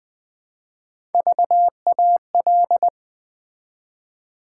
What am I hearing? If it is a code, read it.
Morse code, heard as a keyed tone, about 20 wpm, 700 Hz. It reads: VAL
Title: Build A Word - 1 Letter At A Time - 3 to 5-Letter Words - 20wpm